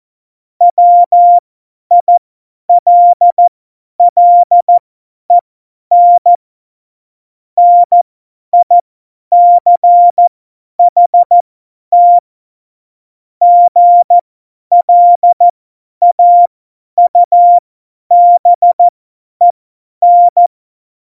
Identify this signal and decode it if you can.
Morse code, heard as a keyed tone, about 14 wpm, spaced as for 10 wpm, 700 Hz. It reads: WILLEN NICHT GLAUBEN
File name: Morsekurs / Witze / Witz 08. Dezember 2020, 50 F70